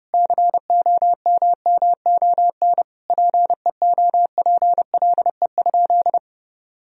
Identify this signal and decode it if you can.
Morse code, heard as a keyed tone, about 32 wpm, 700 Hz. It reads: COMMON PEOPLE?
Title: Sentences from Top 1000 Words - Review Part 3 - 30wpm